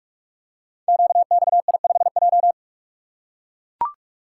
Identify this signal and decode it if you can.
Morse code, heard as a keyed tone, about 44 wpm, 700 Hz. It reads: QXI5J
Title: Code Groups - 5 Letter:Numbers - 45wpm